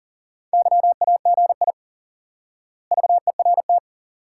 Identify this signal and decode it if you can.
Morse code, heard as a keyed tone, about 37 wpm, 700 Hz. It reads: YAGI VERT
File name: IQE Course 3 - Lesson 89 - YAGI - Two Elements - 40wpm